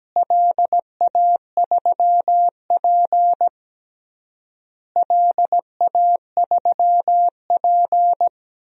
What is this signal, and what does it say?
Morse code, heard as a keyed tone, about 17 wpm, 700 Hz. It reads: LA3P LA3P
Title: DXCC-100 Call Signs - 4 Characters - Repeated 3x - 17wpm